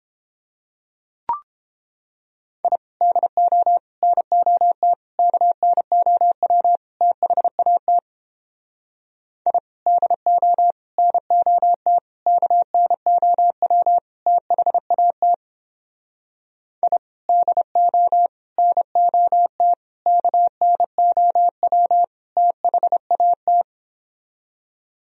Morse code, heard as a keyed tone, about 29 wpm, 700 Hz. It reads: I DO NOT KNOW THAT I DO NOT KNOW THAT I DO NOT KNOW THAT